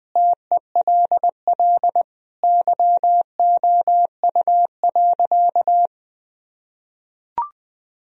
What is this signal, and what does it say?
Morse code, heard as a keyed tone, about 20 wpm, 700 Hz. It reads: TELL YOU.
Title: Sentences from Top 1000 Words - Review Part 3 - 20wpm